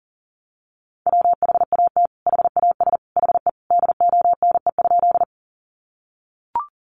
Morse code, heard as a keyed tone, about 40 wpm, 700 Hz. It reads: WHAT HAS HE DONE?
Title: Sentences from Top 500 Words 40wpm